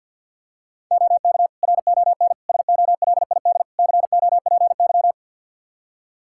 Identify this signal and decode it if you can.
Morse code, heard as a keyed tone, about 52 wpm, 700 Hz. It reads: OK RON SOLID COPY
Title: Practice QSO - Exchange 3 - More Info - 50wpm